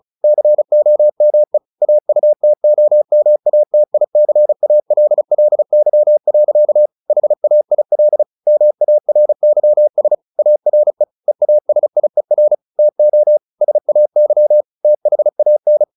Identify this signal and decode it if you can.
Morse code, heard as a keyed tone, about 34 wpm, 600 Hz. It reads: COME AUTOMATICALLY. HAIL MARYS ARE EASIER TO SAY THAN
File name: the old man and the sea / the old man and the sea 1 35wpm